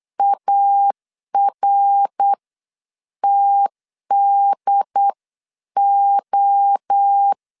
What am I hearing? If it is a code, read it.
Morse code, heard as a keyed tone, about 8 wpm, spaced as for 7 wpm, 800 Hz. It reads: ARTDO
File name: CW / DET / 1 3 1500C S C 8wpm 4214